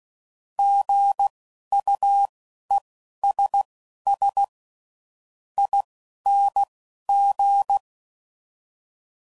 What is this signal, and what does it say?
Morse code, heard as a keyed tone, about 16 wpm, spaced as for 11 wpm, 800 Hz. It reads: GUESS ING
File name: Prefixes / By First Letter